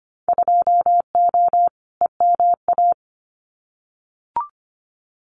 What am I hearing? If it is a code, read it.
Morse code, heard as a keyed tone, about 25 wpm, 700 Hz. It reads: 2O EMA
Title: Winter Field Day Exchanges 25wpm